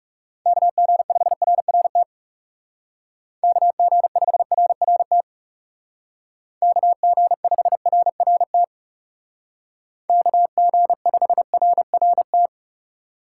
Morse code, heard as a keyed tone, about 37 wpm, 700 Hz. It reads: KG5RRT KG5RRT KG5RRT KG5RRT
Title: Speed Racing - USA Call Signs 30wpm